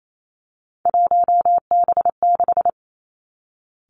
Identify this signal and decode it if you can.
Morse code, heard as a keyed tone, about 28 wpm, 700 Hz. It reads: 1B6
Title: Three Challenging Letter-Number Combinations 28wpm